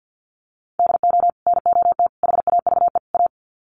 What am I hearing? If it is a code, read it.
Morse code, heard as a keyed tone, about 50 wpm, 700 Hz. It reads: DO NOT HAVE A
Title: Sets of 4 Words 50wpm